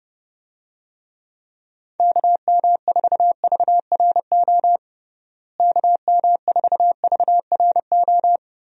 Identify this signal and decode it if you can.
Morse code, heard as a keyed tone, about 30 wpm, 700 Hz. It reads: KM4VRO KM4VRO
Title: USA Call Signs - 6 Characters - Repeated 3x - 30wpm